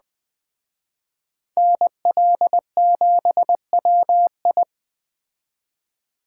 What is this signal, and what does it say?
Morse code, heard as a keyed tone, about 20 wpm, 700 Hz. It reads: NL7WI